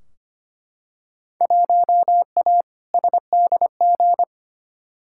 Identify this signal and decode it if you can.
Morse code, heard as a keyed tone, about 25 wpm, 700 Hz. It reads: 1A SDG
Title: Field Day Exchanges Encore - 25wpm